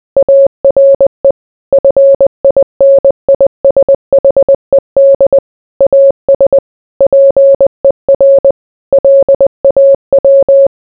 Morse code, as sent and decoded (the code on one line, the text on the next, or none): .- .-. / ..-. .. -. .. ... .... -.. / .- ... / .--. .-. / .-.. .- .--
ARE FINISHED AS PER LAW